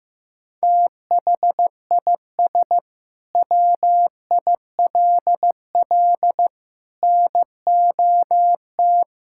Morse code, heard as a keyed tone, about 15 wpm, 700 Hz. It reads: THIS WILL NOT
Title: Sentences from Top 1000 Words - Review Part 6 - 15wpm